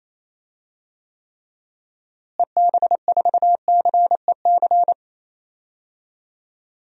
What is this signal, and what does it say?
Morse code, heard as a keyed tone, about 27 wpm, 700 Hz. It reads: EB4CEC
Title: DXCC-100 Call Signs - 6 Characters - Repeated 3x - 28wpm